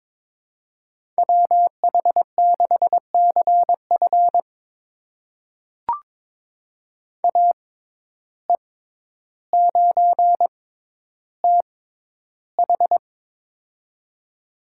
Morse code, heard as a keyed tone, about 22 wpm, spaced as for 11 wpm, 700 Hz. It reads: WH6CF AE9TH
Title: USA Call Signs - 5 Characters - 6x Character Spacing - 22wpm